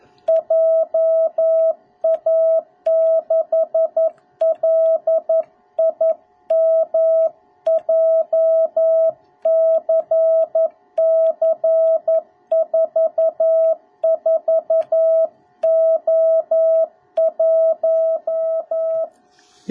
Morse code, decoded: JA6LIM JCC44O1

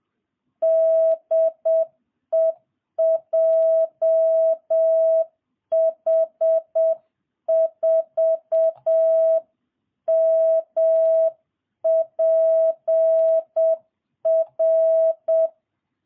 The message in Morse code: DE JH4MPR

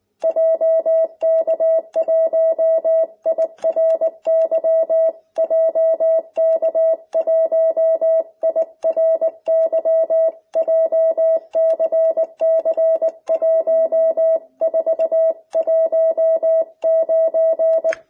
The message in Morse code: JK1IRY JK1IRY JCC1419

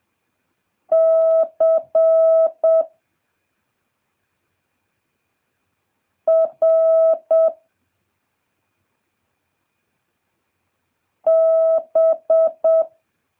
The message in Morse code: C R B

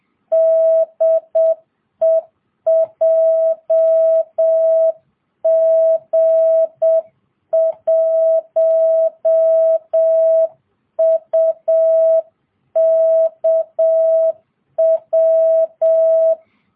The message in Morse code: DE JG1UKW

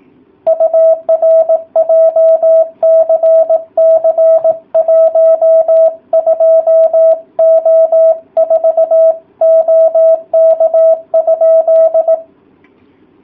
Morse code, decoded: UR JCC 12O4 OK?

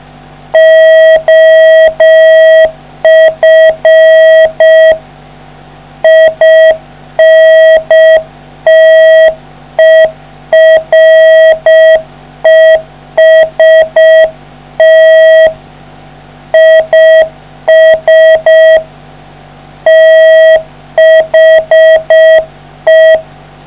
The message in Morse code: OF INTEREST IS THE

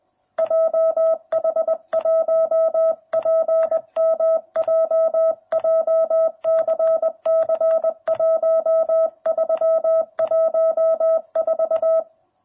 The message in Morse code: JH1PMJ JCC 1314

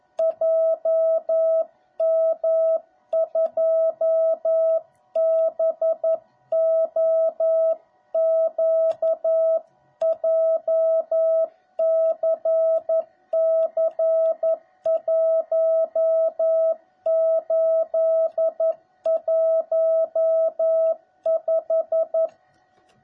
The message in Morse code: JM2BOQ JCC1815

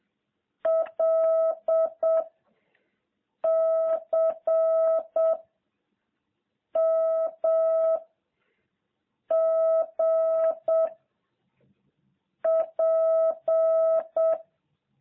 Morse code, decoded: L C M G P

